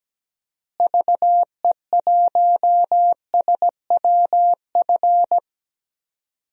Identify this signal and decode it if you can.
Morse code, heard as a keyed tone, about 17 wpm, 700 Hz. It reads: VE1SWF
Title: DXCC-100 Call Signs - 6 Characters - Repeated 2x - 17wpm